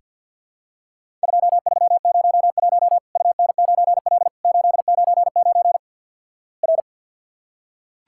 Morse code, heard as a keyed tone, about 48 wpm, 700 Hz. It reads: 2301 UN9L 899 R
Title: Worked All Europe DX Contest - QTC Exchange - 50wpm